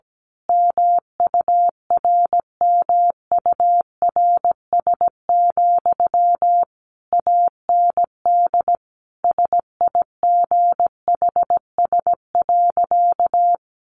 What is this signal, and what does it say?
Morse code, heard as a keyed tone, about 17 wpm, 700 Hz. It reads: MURMURS, AND SIGHS.
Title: A Sound Of Thunder 17wpm